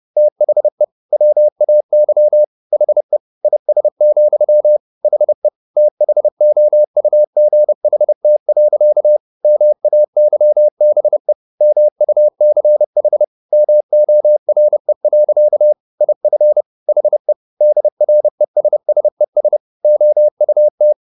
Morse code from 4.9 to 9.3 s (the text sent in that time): HE THOUGHT.